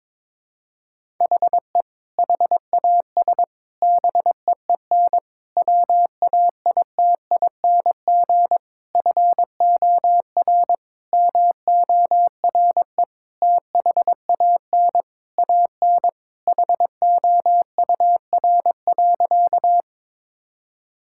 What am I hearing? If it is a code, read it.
Morse code, heard as a keyed tone, about 21 wpm, 700 Hz. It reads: HE HAS BEEN WAITING FOR MORE THAN AN HOUR.
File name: practice / 22wpm / Sentences with 3-Word Phrases - 22wpm